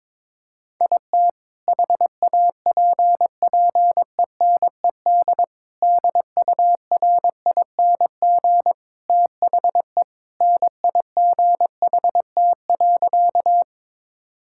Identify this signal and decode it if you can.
Morse code, heard as a keyed tone, about 21 wpm, 700 Hz. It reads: IT HAPPENED DURING THE NIGHT.
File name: practice / 22wpm / Sentences with 2-Word Phrases - 22wpm